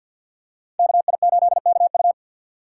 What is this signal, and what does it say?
Morse code, heard as a keyed tone, about 48 wpm, 700 Hz. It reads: KI8KU
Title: USA Call Signs - 5 Characters - 4x Character Spacing - 50wpm